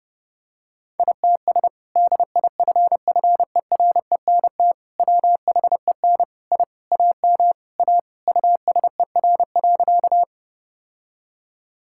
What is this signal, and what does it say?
Morse code, heard as a keyed tone, about 28 wpm, 700 Hz. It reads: ITS DIFFERENT WHEN I AM A USER.
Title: Sentences from Top 1000 Words - Review Part 5 - 30wpm